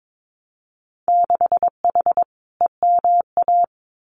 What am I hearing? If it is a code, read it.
Morse code, heard as a keyed tone, about 22 wpm, 700 Hz. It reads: 6H EMA
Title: Winter Field Day Exchanges Encore 22wpm